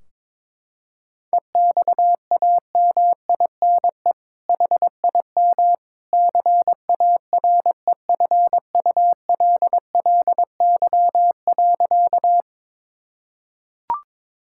Morse code, heard as a keyed tone, about 22 wpm, 700 Hz. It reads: EXAMINE HIM CAREFULLY.
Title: Sentences from Top 1600 Words 22wpm